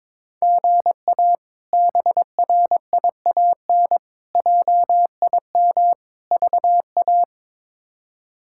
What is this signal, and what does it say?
Morse code, heard as a keyed tone, about 22 wpm, 700 Hz. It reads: GA BRIAN JIM VA
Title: K1USN SST QSO Practice - 22wpm